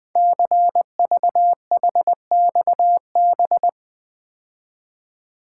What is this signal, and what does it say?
Morse code, heard as a keyed tone, about 20 wpm, 700 Hz. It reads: CVHXB
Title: Code Groups - 5 Letters - 20wpm